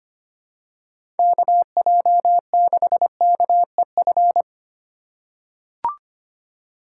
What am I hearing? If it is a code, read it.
Morse code, heard as a keyed tone, about 25 wpm, 700 Hz. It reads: KJ6KEF
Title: USA Call Signs - 6 Characters - 4x Character Spacing - 25wpm